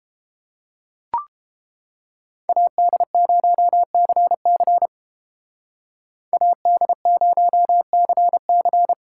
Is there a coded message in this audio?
Yes: Morse code from 2.5 to 8.9 s, heard as a keyed tone, about 32 wpm, 700 Hz. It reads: AD0CC AD0CC